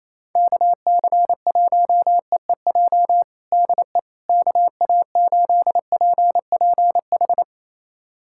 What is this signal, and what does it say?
Morse code, heard as a keyed tone, about 28 wpm, 700 Hz. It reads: KC1EEJ DE KA8PPH
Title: USA Callsign DE Callsign - 6 Characters - 28wpm